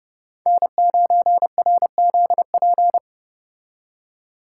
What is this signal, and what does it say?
Morse code, heard as a keyed tone, about 29 wpm, 700 Hz. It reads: N9RZP